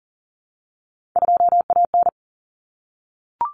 Morse code, heard as a keyed tone, about 40 wpm, 700 Hz. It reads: 2AN
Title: Common Cut Numbers - 3-Digit 40wpm